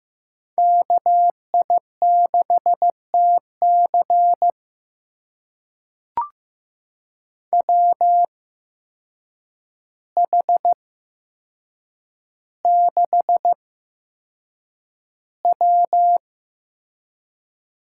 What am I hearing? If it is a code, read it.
Morse code, heard as a keyed tone, about 15 wpm, spaced as for 7 wpm, 700 Hz. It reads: KI6TC WH6W
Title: USA Call Signs - 5 Characters - 8x Character Spacing - 15wpm